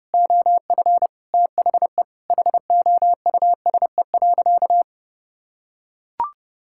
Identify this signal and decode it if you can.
Morse code, heard as a keyed tone, about 30 wpm, 700 Hz. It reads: OF THE HOUSE.
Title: Sentences from Top 1000 Words - Review Part 5 - 30wpm